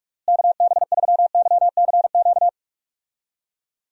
Morse code, heard as a keyed tone, about 46 wpm, 700 Hz. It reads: KB3YCQ
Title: USA Call Signs - 6 Characters - Repeated 3x - 45wpm